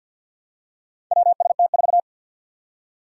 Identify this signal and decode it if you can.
Morse code, heard as a keyed tone, about 50 wpm, 700 Hz. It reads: WST4
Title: Four Challenging Letter-Number Combinations 50wpm